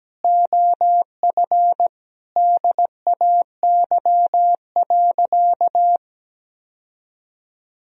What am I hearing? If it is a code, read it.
Morse code, heard as a keyed tone, about 17 wpm, 700 Hz. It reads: OF DAY.
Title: Sentences from Top 1000 Words - Review Part 4 - 17wpm